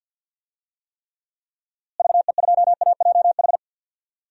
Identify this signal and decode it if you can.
Morse code, heard as a keyed tone, about 53 wpm, 700 Hz. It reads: VE2AJH
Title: DXCC-100 Call Signs - 6 Characters - Repeated 2x - 50wpm